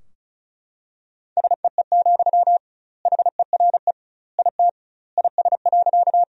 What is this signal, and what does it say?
Morse code, heard as a keyed tone, about 34 wpm, 700 Hz. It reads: SEE, HERE IT IS.
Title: 2x Word-Spacing - Sentences from Top 200 Words - 35wpm